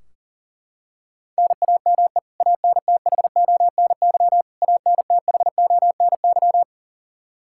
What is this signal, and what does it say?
Morse code, heard as a keyed tone, about 40 wpm, 700 Hz. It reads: NAME ANTHONY ANTHONY
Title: Practice QSO - Exchange 1 - Beginning - 40wpm